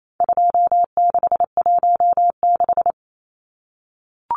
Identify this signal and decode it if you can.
Morse code, heard as a keyed tone, about 28 wpm, 700 Hz. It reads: 2616
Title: Numbers - 4-Digit 28wpm